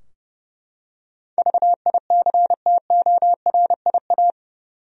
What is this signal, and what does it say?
Morse code, heard as a keyed tone, about 30 wpm, 700 Hz. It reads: VICTORIA
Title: USA Common Names - 30wpm